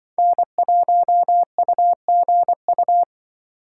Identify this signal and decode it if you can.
Morse code, heard as a keyed tone, about 24 wpm, 700 Hz. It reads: N1UGU
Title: Speed Racing - USA 1x3 Call Signs 20wpm